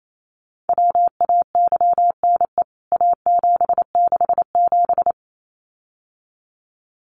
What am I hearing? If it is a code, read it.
Morse code, heard as a keyed tone, about 28 wpm, 700 Hz. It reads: WAYNE A767
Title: CWT Exchange 28wpm